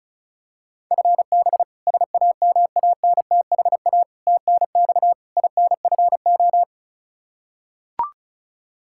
Morse code, heard as a keyed tone, about 34 wpm, 700 Hz. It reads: FB SAMANTHA TNX INFO
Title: K1USN SST QSO Practice 35wpm